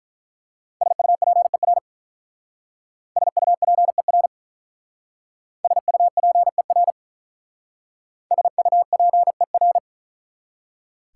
Morse code, heard as a keyed tone, about 44 wpm, 700 Hz. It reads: SUPER SUPER SUPER SUPER